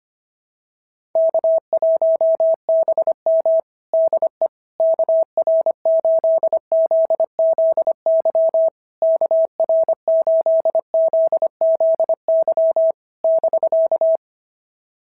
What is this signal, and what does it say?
Morse code, heard as a keyed tone, about 22 wpm, 650 Hz. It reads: K1BM DE KR8ZZY KR8ZZY <BK>
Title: SOTA QSO Practice 25wpm